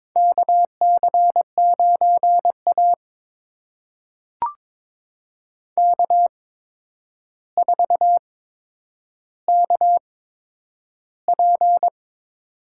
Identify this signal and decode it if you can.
Morse code, heard as a keyed tone, about 21 wpm, spaced as for 9 wpm, 700 Hz. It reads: KC9A K4KP
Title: USA Call Signs - 4 Characters - 8x Character Spacing - 22wpm